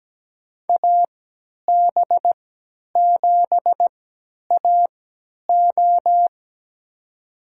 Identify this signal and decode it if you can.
Morse code, heard as a keyed tone, about 17 wpm, spaced as for 10 wpm, 700 Hz. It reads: AB7AO
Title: USA Call Signs - 5 Characters - 3x Character Spacing - 17wpm